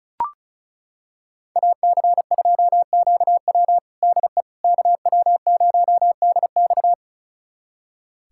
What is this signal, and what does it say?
Morse code, heard as a keyed tone, about 36 wpm, 700 Hz. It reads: AC2QW DE KW0DX
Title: USA Callsign DE Callsign - 5 Characters - 35wpm